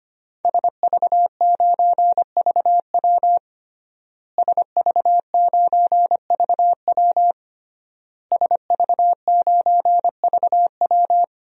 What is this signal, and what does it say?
Morse code, heard as a keyed tone, about 25 wpm, 700 Hz. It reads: SV9VW SV9VW SV9VW
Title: DXCC-100 Call Signs - 5 Characters - Repeated 3x - 25wpm